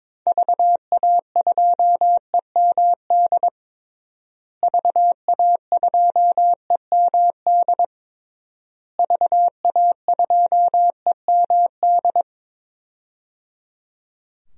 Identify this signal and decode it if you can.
Morse code, heard as a keyed tone, about 22 wpm, 700 Hz. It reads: VA2EMD VA2EMD VA2EMD
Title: DXCC-100 Call Signs - 6 Characters - Repeated 3x - 22wpm